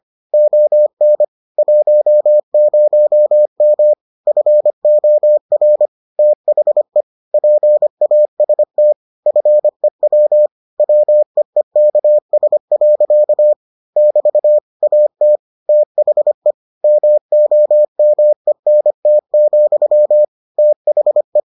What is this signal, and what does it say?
Morse code, heard as a keyed tone, about 25 wpm, 600 Hz. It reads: ON 10M FOR THE PAST FEW WEEKS. <BT> AT THE MOMENT, THE